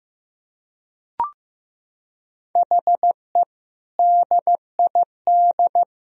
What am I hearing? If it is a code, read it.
Morse code, heard as a keyed tone, about 15 wpm, 700 Hz. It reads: HE DID
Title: Sentences from Top 1000 Words - Review Part 1 - 15wpm